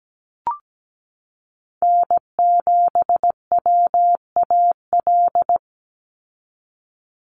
Morse code, heard as a keyed tone, about 17 wpm, 700 Hz. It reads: N7WAL
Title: USA Call Signs 17wpm